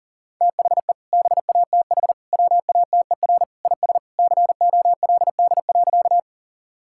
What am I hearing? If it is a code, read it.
Morse code, heard as a keyed tone, about 40 wpm, 700 Hz. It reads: THE BATH WATER IS COLD.
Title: Sentences from Top 1600 Words 40wpm